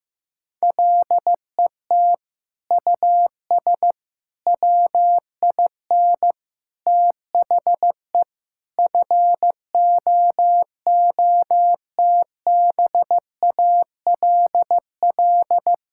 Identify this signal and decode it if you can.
Morse code, heard as a keyed tone, about 16 wpm, 700 Hz. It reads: LET US WIN THE FOOTBALL
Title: Sentences from Top 1600 Words 15wpm